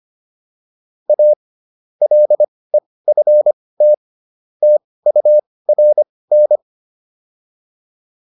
Morse code, heard as a keyed tone, about 25 wpm, spaced as for 18 wpm, 600 Hz. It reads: A LEFT TURN